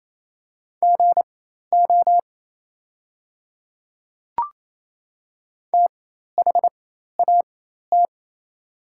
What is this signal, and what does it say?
Morse code, heard as a keyed tone, about 28 wpm, spaced as for 13 wpm, 700 Hz. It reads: GO THAT
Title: 4x Character Spacing - Top 200 Words - 28wpm